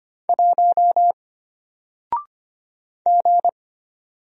Morse code, heard as a keyed tone, about 25 wpm, 700 Hz. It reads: 1 G
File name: Single-Letter-Number-25wpm